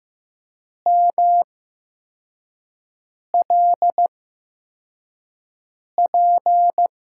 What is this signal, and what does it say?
Morse code, heard as a keyed tone, about 15 wpm, spaced as for 4 wpm, 700 Hz. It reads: MLP
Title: USA Call Signs - 5 Characters - 8x Character Spacing - 15wpm